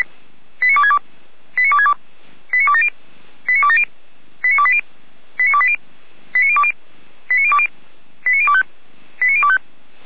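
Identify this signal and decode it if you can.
Five-tone selective call (ZVEI), heard as quick runs of five tones, tones of about 70 ms. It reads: 89262 89262 89270 89270 89280 89280 89020 89020 89025 89025
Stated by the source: ZVEI five-tone selective call received by SDR.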